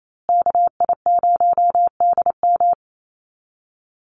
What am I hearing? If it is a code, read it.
Morse code, heard as a keyed tone, about 28 wpm, 690 Hz. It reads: KI0DM
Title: USA Call Signs Repeated 3x 28wpm